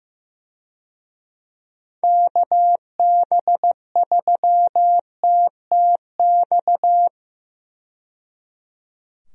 Morse code, heard as a keyed tone, about 16 wpm, 700 Hz. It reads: KB3TTX
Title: USA Call Signs - 6 Characters - 15wpm